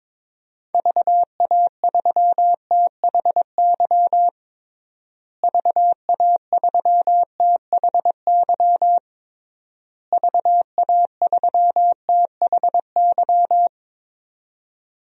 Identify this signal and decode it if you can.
Morse code, heard as a keyed tone, about 22 wpm, 700 Hz. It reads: VA3THY VA3THY VA3THY
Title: DXCC-100 Call Signs - 6 Characters - Repeated 3x - 22wpm